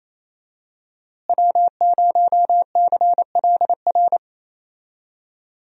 Morse code, heard as a keyed tone, about 27 wpm, 700 Hz. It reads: W0CLR
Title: USA Call Signs - 5 Characters - Repeated 3x - 28wpm